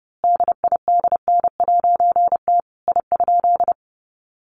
Morse code, heard as a keyed tone, about 30 wpm, 700 Hz. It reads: DIDN'T I?
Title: A Sound Of Thunder 30wpm